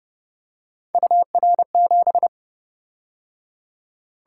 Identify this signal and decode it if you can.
Morse code, heard as a keyed tone, about 28 wpm, 700 Hz. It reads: UR7